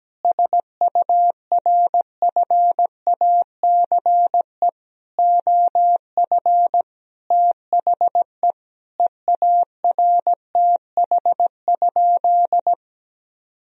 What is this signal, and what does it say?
Morse code, heard as a keyed tone, about 17 wpm, 700 Hz. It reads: SURFACE OF THE EARTH?
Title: Sentences from Top 1000 Words - Review Part 3 - 17wpm